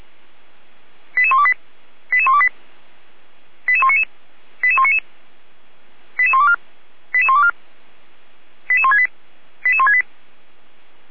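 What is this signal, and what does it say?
Five-tone selective call (ZVEI), heard as quick runs of five tones, tones of about 70 ms. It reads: 88128 88128 88199 88199 88124 88124 88168 88168
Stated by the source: ZVEI five-tone selective call 88122 received on SDR, 455 channel.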